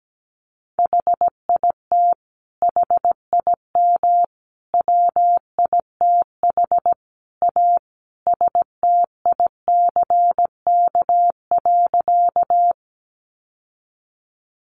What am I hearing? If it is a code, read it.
Morse code, heard as a keyed tone, about 17 wpm, 700 Hz. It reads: HIT HIM WITH A STICK.
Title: Sentences from Top 1200 Words 17wpm